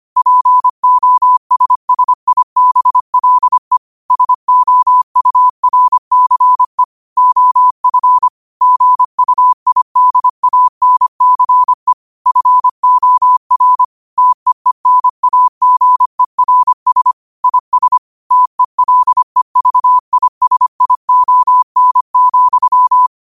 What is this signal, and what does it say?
Morse code, heard as a keyed tone, about 25 wpm, 1000 Hz. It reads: POSSIBLE SOURCE OF GUIDANCE FOR TEENAGERS IS TELEVISION,
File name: 25wpm